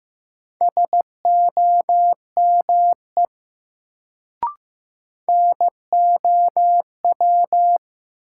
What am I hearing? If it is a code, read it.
Morse code, heard as a keyed tone, about 15 wpm, 700 Hz. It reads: SOME NOW
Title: Sets of 2 Words - New Word Drill - 15wpm